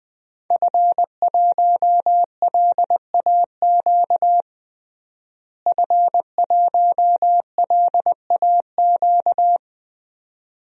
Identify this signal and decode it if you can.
Morse code, heard as a keyed tone, about 20 wpm, 700 Hz. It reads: F1LAQ F1LAQ
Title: DXCC-100 Call Signs - 5 Characters - Repeated 2x - 20wpm